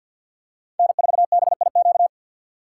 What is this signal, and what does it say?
Morse code, heard as a keyed tone, about 53 wpm, 700 Hz. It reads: N4BIQ
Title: USA Call Signs - 5 Characters - 2x Character Spacing - 50wpm